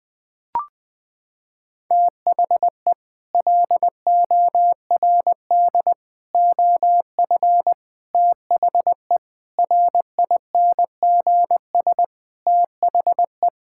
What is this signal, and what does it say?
Morse code, heard as a keyed tone, about 20 wpm, 700 Hz. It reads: THE LORD OF THE RINGS THE
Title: Sets of 9 Words - 20wpm